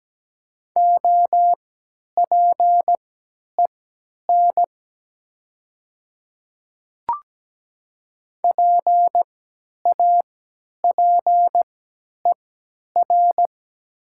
Morse code, heard as a keyed tone, about 17 wpm, spaced as for 10 wpm, 700 Hz. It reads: OPEN PAPER
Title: 3x Character Spacing - Top 500 Words - 17wpm